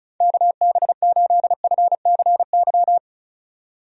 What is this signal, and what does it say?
Morse code, heard as a keyed tone, about 34 wpm, 700 Hz. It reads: KB8FCY
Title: USA Call Signs - 6 Characters - 35wpm